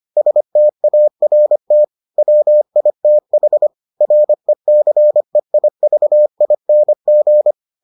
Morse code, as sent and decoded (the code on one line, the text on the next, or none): ... - .- .-. - / .-- .. - .... / .-. -.-. .. ...- .. -. --.
START WITH RECEIVING